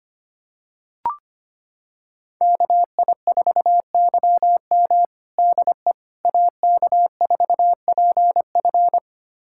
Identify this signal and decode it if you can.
Morse code, heard as a keyed tone, about 25 wpm, 700 Hz. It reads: KI4YM DE AK4PF
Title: USA Callsign DE Callsign - 5 Characters - 25wpm